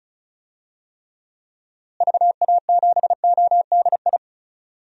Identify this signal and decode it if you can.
Morse code, heard as a keyed tone, about 36 wpm, 700 Hz. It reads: VA7ODI
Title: DXCC-100 Call Signs - 6 Characters - Repeated 2x - 35wpm